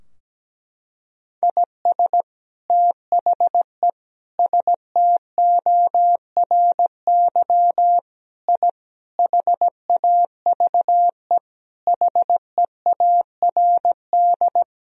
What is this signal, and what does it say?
Morse code, heard as a keyed tone, about 17 wpm, 700 Hz. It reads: IS THE STORY I HAVE HEARD
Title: Sentences from Top 1000 Words - Review Part 2 - 17wpm